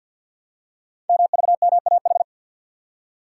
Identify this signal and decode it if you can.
Morse code, heard as a keyed tone, about 52 wpm, 700 Hz. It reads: MVMAH
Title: Code Groups - 5 Letters - 50wpm